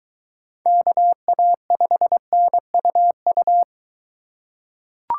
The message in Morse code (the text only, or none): KA5NUU